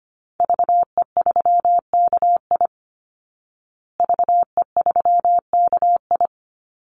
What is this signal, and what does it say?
Morse code, heard as a keyed tone, about 25 wpm, 700 Hz. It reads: VE3KI VE3KI
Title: Call Signs Repeated 2x - 25wpm